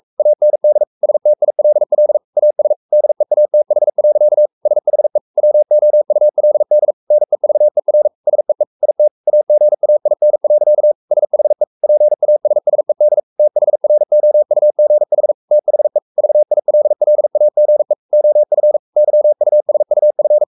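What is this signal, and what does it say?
Morse code, heard as a keyed tone, about 44 wpm, 600 Hz. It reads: AND STILL AS DEATH. SHE WOULD NEVER SEE IT AGAIN. SHE PASSED THROUGH THE VILLAGE OF YUSUF
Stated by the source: Morse code at 43 wpm sending Ken Follett's novel Never in English, from CW med Gnister.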